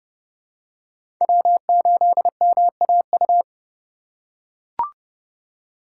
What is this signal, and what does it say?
Morse code, heard as a keyed tone, about 32 wpm, 700 Hz. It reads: W8MAU
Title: USA Call Signs - 5 Characters - 3x Character Spacing - 30wpm